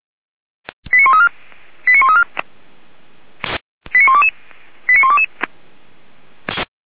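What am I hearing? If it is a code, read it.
Five-tone selective call (ZVEI), heard as quick runs of five tones, tones of about 70 ms. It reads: 89134 89134 89122 89122
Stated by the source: ZVEI five-tone call received by SDR, from the feuerwehr-orlamuende.org fire brigade audio archive.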